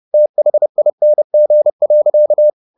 Morse code, heard as a keyed tone, about 30 wpm, 600 Hz. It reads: THING.
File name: the old man and the sea / the old man and the sea 1 30wpm